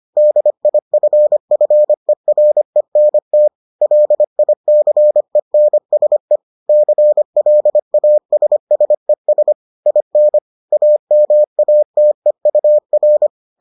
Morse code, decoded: DIFFERENT LICENSE CLASSES IN AMATEUR

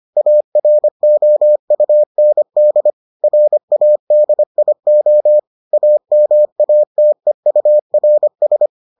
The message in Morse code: AROUND RADIO AMATEURS

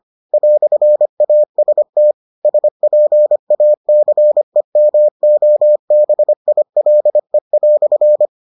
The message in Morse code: "AST SPACEMOBILE"

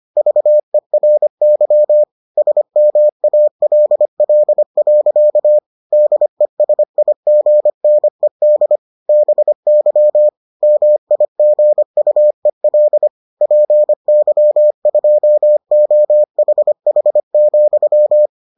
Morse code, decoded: VERY SMALL. DESIGNED BY MIGUEL PY2OHH,

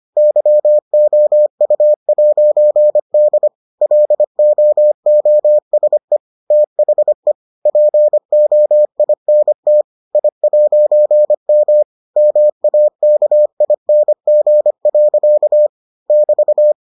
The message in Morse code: YOU'D LOOSE THE POINT I'M MAKING. <BT>

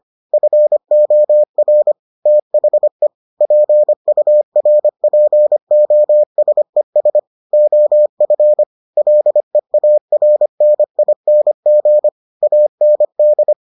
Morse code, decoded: FOR THE PURPOSES OF LEARNING AND

